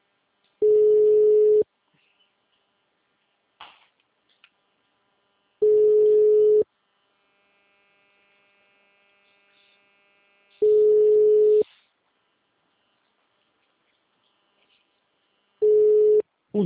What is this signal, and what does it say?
Ringback tone (Europe, 425 Hz), with speech.
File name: out- -2002- - - .623